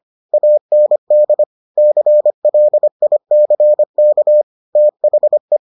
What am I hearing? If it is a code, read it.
Morse code, heard as a keyed tone, about 25 wpm, 600 Hz. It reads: AND CLICK THE